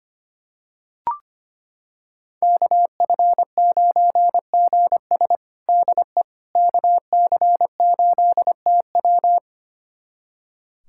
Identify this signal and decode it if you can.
Morse code, heard as a keyed tone, about 25 wpm, 700 Hz. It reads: KF9GS DE KC8TW
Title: USA Callsign DE Callsign - 5 Characters - 25wpm